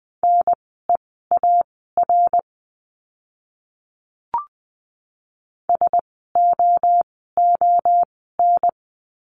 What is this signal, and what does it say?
Morse code, heard as a keyed tone, about 20 wpm, spaced as for 14 wpm, 700 Hz. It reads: NEAR SOON
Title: 054 Lesson 10f - Letter r - Words 20wpm